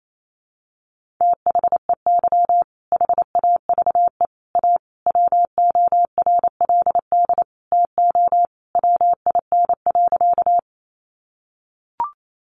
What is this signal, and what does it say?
Morse code, heard as a keyed tone, about 28 wpm, 700 Hz. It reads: THEY HAVE A WORLD TO WIN.
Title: Sentences from Top 1500 Words 28wpm